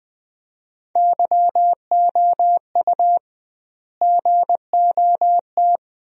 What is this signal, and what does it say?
Morse code, heard as a keyed tone, about 20 wpm, 700 Hz. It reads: YOU GOT
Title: 2x Word-Spacing - Sentences from Top 300 Words - 20wpm